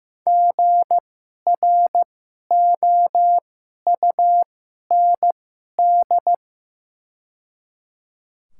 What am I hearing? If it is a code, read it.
Morse code, heard as a keyed tone, about 15 wpm, spaced as for 11 wpm, 700 Hz. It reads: GROUND
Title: 2x Character Spacing - Top 500 Words - 15wpm